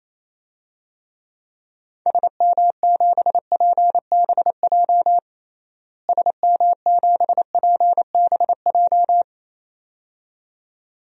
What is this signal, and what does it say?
Morse code, heard as a keyed tone, about 28 wpm, 700 Hz. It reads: SM7PBJ SM7PBJ
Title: DXCC-100 Call Signs - 6 Characters - Repeated 2x - 28wpm